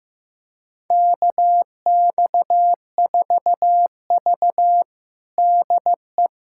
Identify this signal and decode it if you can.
Morse code, heard as a keyed tone, about 16 wpm, 700 Hz. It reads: KX4V DE
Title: USA Callsign DE Callsign 15wpm